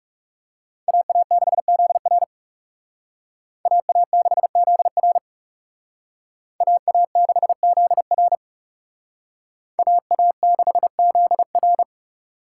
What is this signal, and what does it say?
Morse code, heard as a keyed tone, about 37 wpm, 700 Hz. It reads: AA6ZR AA6ZR AA6ZR AA6ZR
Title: Speed Racing - USA Call Signs 30wpm